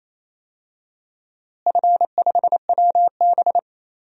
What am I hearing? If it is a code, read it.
Morse code, heard as a keyed tone, about 28 wpm, 700 Hz. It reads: F5WB
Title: DXCC-100 Call Signs - 4 Characters - Repeated 2x - 28wpm